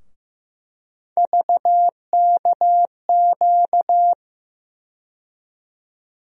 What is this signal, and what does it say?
Morse code, heard as a keyed tone, about 15 wpm, 700 Hz. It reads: VKQ
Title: USA Call Sign Postfixes - 3 Letters 15wpm